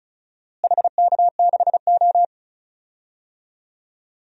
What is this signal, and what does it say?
Morse code, heard as a keyed tone, about 36 wpm, 700 Hz. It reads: HK6O
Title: DXCC-100 Call Signs - 4 Characters - Repeated 2x - 35wpm